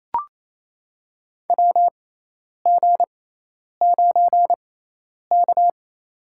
Morse code, heard as a keyed tone, about 28 wpm, spaced as for 10 wpm, 700 Hz. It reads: WG9K
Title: USA Call Signs - 4 Characters - 6x Character Spacing - 28wpm